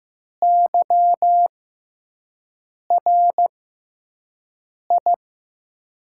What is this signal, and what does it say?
Morse code, heard as a keyed tone, about 15 wpm, spaced as for 5 wpm, 700 Hz. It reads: YRI
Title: USA Call Signs - 6x Character Spacing - 15wpm